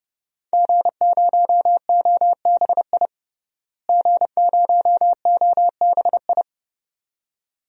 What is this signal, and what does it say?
Morse code, heard as a keyed tone, about 30 wpm, 700 Hz. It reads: G0OBI G0OBI
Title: DXCC-100 Call Signs - 5 Characters - Repeated 2x - 30wpm